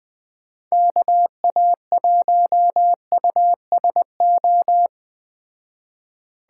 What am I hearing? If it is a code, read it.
Morse code, heard as a keyed tone, about 20 wpm, 700 Hz. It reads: KA1USO
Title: Speed Racing - USA Call Signs 17wpm